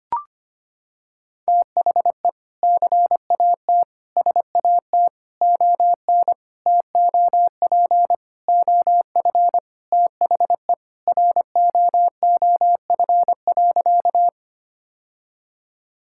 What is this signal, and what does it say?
Morse code, heard as a keyed tone, about 25 wpm, 700 Hz. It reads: THE CAT SAT ON TOP OF THE ROOF.
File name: practice / 25wpm / Sentences with 3-Word Phrases - 25wpm